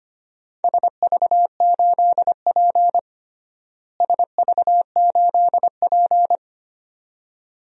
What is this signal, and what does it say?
Morse code, heard as a keyed tone, about 25 wpm, 700 Hz. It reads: SV8P SV8P
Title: DXCC-100 Call Signs - 4 Characters - Repeated 2x - 25wpm